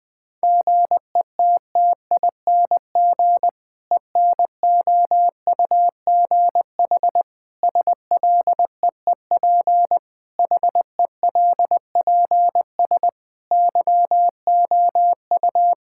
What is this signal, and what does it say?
Morse code, heard as a keyed tone, about 21 wpm, 700 Hz. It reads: GETTING ENOUGH SLEEP HELPS YOU
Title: Difficult Sentences with Hints - 20wpm